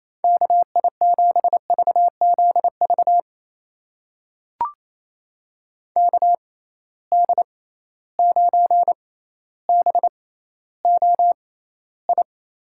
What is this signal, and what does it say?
Morse code, heard as a keyed tone, about 28 wpm, spaced as for 14 wpm, 700 Hz. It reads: KI7VZV KD9BOI
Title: USA Call Signs - 6 Characters - 6x Character Spacing - 28wpm